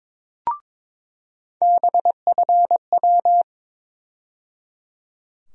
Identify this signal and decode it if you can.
Morse code, heard as a keyed tone, about 22 wpm, 700 Hz. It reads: BFW
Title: USA Call Sign Postfixes - 3 Letters 22wpm